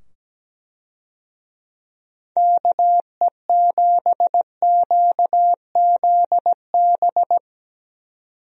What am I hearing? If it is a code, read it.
Morse code, heard as a keyed tone, about 17 wpm, 700 Hz. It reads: KE7QZB
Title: USA Call Signs - 6 Characters - 17wpm